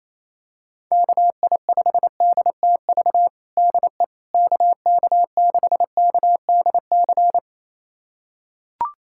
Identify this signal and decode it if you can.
Morse code, heard as a keyed tone, about 28 wpm, 700 Hz. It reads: KI5DTV DE KK6KDC
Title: USA Callsign DE Callsign - 6 Characters - 28wpm